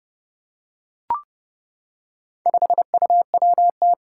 Morse code, heard as a keyed tone, about 29 wpm, 700 Hz. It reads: 5UWT
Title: Four Challenging Letter-Number Combinations 30wpm